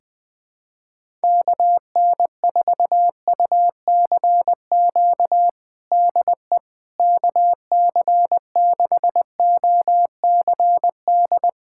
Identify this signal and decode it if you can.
Morse code, heard as a keyed tone, about 20 wpm, 700 Hz. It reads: KN4UCQ DE KC6OCD
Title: USA Callsign DE Callsign - 6 Characters - 20wpm